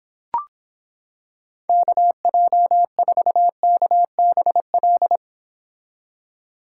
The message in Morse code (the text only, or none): KJ4KBL